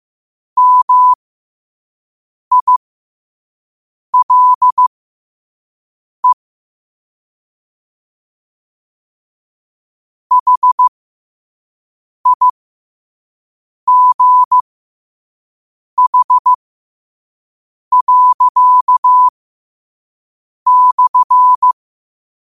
Morse code, heard as a keyed tone, about 15 wpm, spaced as for 5 wpm, 1000 Hz. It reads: MILE HIGH./
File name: qotdfiles / 5wpm